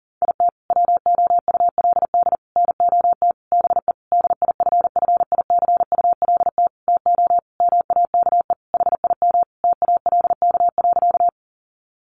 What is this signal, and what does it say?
Morse code, heard as a keyed tone, about 40 wpm, 700 Hz. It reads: IT WOULD NOT BE DIFFICULT TO MAKE HIM TALK.
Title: Sentences from Top 600 Words 40wpm